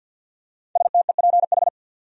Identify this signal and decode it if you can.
Morse code, heard as a keyed tone, about 51 wpm, 700 Hz. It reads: STEPH
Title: USA Common Names - Max 5 Letters - 50wpm